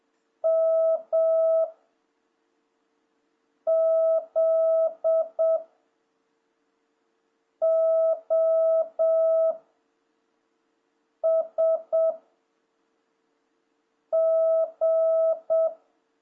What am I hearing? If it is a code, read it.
Morse code, heard as a keyed tone, about 7 wpm, 660 Hz. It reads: M Z O S G
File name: jh8cbh / 2026cw